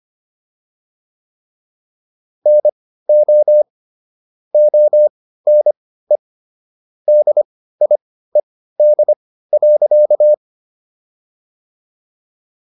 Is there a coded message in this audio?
Yes: Morse code from 2.5 to 10.3 s, heard as a keyed tone, about 25 wpm, spaced as for 15 wpm, 600 Hz. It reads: NO ONE DIED.